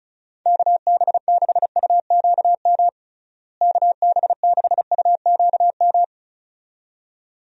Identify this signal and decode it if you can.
Morse code, heard as a keyed tone, about 36 wpm, 700 Hz. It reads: KB6UQM KB6UQM
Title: USA Call Signs - 6 Characters - Repeated 2x - 35wpm